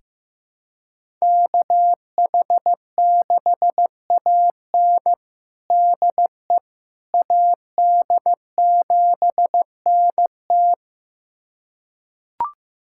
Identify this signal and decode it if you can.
Morse code, heard as a keyed tone, about 16 wpm, 700 Hz. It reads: KH6AN DE AD7NT
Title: USA Callsign DE Callsign 15wpm